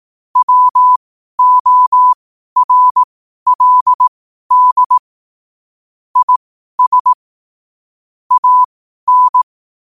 Morse code, heard as a keyed tone, about 18 wpm, spaced as for 12 wpm, 1000 Hz. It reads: WORLD IS AN
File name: qotdfiles / 12wpm